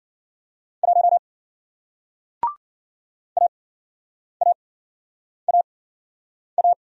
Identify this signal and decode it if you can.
Morse code, heard as a keyed tone, about 54 wpm, spaced as for 9 wpm, 700 Hz. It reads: 1 AAAA